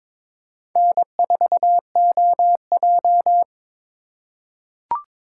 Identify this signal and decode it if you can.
Morse code, heard as a keyed tone, about 22 wpm, 700 Hz. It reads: N4OJ
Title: USA Call Signs - 4 Characters - 4x Character Spacing - 22wpm